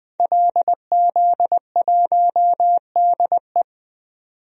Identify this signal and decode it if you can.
Morse code, heard as a keyed tone, about 20 wpm, 700 Hz. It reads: LZ1DE
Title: DXCC-100 Call Signs - 5 Characters - 20wpm